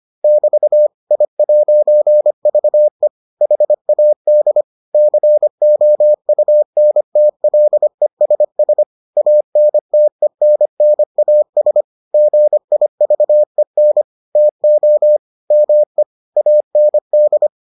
Morse code, heard as a keyed tone, about 25 wpm, 600 Hz. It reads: <BT> I'VE HAD COUNTLESS ANTENNAS GIVEN TO ME AND